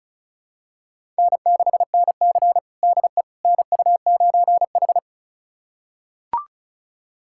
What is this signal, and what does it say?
Morse code, heard as a keyed tone, about 36 wpm, 700 Hz. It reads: N6NC DE NU9H
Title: USA Callsign DE Callsign - 4 Characters - 35wpm